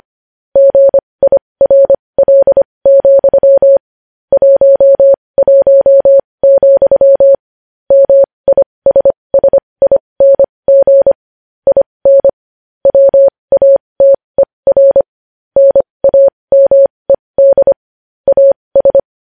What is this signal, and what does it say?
Morse code, heard as a keyed tone, about 25 wpm, spaced as for 20 wpm, 550 Hz. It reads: GIRL, 11, MISSING IN WATER NAMED AS
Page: news-headlines-in-morse-code-at-20-wpm